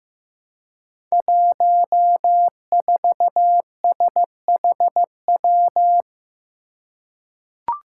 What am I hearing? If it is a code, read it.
Morse code, heard as a keyed tone, about 15 wpm, 700 Hz. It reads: 14SHW
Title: Five Challenging Letter-Number Combinations 15wpm